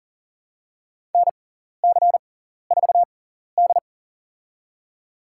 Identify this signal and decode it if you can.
Morse code, heard as a keyed tone, about 40 wpm, spaced as for 14 wpm, 700 Hz. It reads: NC4D
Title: USA Call Signs - 6x Character Spacing - 40wpm